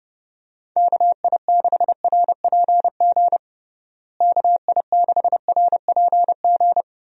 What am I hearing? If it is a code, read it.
Morse code, heard as a keyed tone, about 30 wpm, 700 Hz. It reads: KI6RPG KI6RPG
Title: USA Call Signs - 6 Characters - Repeated 2x - 30wpm